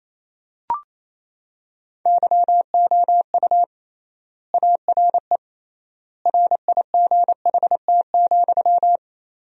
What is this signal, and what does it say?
Morse code, heard as a keyed tone, about 28 wpm, 700 Hz. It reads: YOU ARE RIGHT,
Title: 3x Word-Spacing - Sentences from Top 300 Words - 28wpm